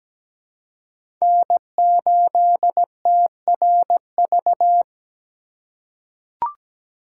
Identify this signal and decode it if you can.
Morse code, heard as a keyed tone, about 17 wpm, 700 Hz. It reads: N8TRV
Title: USA Call Signs - 5 Characters - 2x Character Spacing - 17wpm